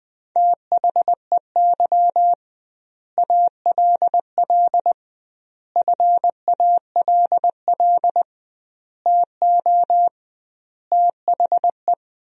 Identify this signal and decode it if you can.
Morse code, heard as a keyed tone, about 20 wpm, 700 Hz. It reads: THEY ALL FALL TO THE
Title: 2x Word-Spacing - Sentences from Top 500 Words - 20wpm